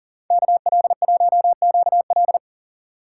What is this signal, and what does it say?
Morse code, heard as a keyed tone, about 48 wpm, 700 Hz. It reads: KL1QL
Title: Speed Racing - USA 2x2 Call Signs 40wpm